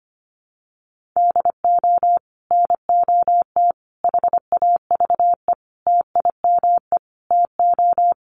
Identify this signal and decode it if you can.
Morse code, heard as a keyed tone, about 25 wpm, 700 Hz. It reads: DO NOT HAVE TIME TO
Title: Sets of 5 Words 25wpm